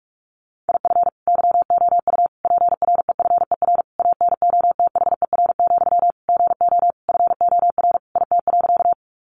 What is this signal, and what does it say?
Morse code, heard as a keyed tone, about 44 wpm, 700 Hz. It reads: IF YOU PREFER ANOTHER, GO FOR IT.
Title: Sentences from Top 1100 Words 45wpm